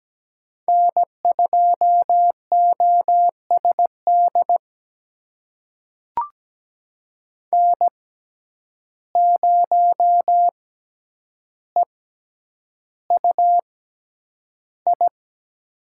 Morse code, heard as a keyed tone, about 17 wpm, spaced as for 9 wpm, 700 Hz. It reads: N2OSD N0EUI
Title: USA Call Signs - 5 Characters - 6x Character Spacing - 17wpm